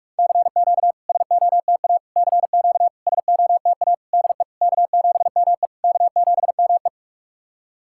Morse code, heard as a keyed tone, about 48 wpm, 690 Hz. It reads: CQ SOTA CQ SOTA DE K7ME K7ME
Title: SOTA QSO Practice 45wpm